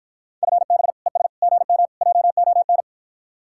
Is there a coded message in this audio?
Yes: Morse code from 0.4 to 2.8 s, heard as a keyed tone, about 60 wpm, 700 Hz. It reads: FB ES GM JON